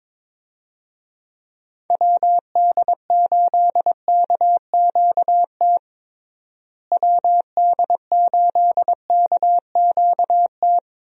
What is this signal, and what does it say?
Morse code, heard as a keyed tone, about 22 wpm, 700 Hz. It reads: WD8KQT WD8KQT
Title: USA Call Signs - 6 Characters - Repeated 3x - 22wpm